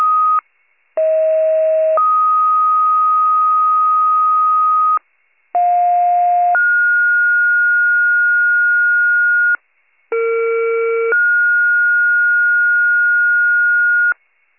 Two-tone page, heard as a one-second tone, then a three-second tone, about 640 then 1250 Hz, about 710 then 1430 Hz, about 460 then 1430 Hz, with noise. It is heard from 1.0 to 14.1 s.
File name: pager / 38 FIRE / wcalert